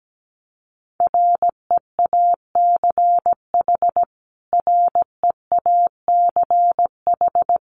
Morse code, heard as a keyed tone, about 17 wpm, 700 Hz. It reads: REACH REACH
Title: Top 500 Words Repeated 17wpm